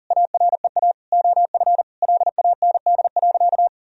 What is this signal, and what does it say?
Morse code, heard as a keyed tone, about 40 wpm, 700 Hz. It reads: AREA OF LAND.
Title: Sentences from Top 1000 Words - Review Part 2 - 40wpm